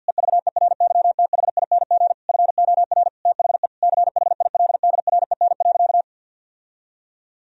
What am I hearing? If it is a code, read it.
Morse code, heard as a keyed tone, about 50 wpm, 700 Hz. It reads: EVERYTHING FOR THE CHILDREN.
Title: Sentences from Top 1000 Words - Review Part 2 - 50wpm